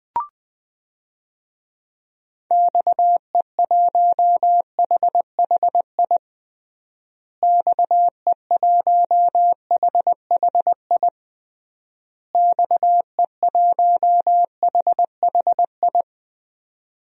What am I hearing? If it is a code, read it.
Morse code, heard as a keyed tone, about 20 wpm, 700 Hz. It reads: XE1HHI XE1HHI XE1HHI
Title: DXCC-100 Call Signs - 6 Characters - Repeated 3x - 20wpm